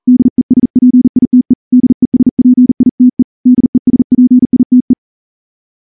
Morse code, heard as a keyed tone, about 39 wpm, 250 Hz. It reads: DESPITE DESPITE DESPITE
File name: morse-training-40WPM